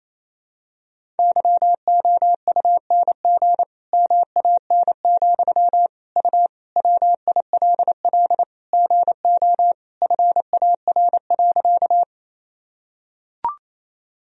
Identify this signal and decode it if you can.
Morse code, heard as a keyed tone, about 28 wpm, 700 Hz. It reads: YOUNG MAN, U WILL GO FAR.
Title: Sentences from Top 400 Words and Abbreviations 28wpm